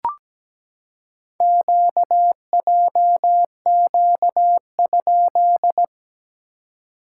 Morse code, heard as a keyed tone, about 17 wpm, 700 Hz. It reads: QJQ?
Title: SOTA QSO Practice 17wpm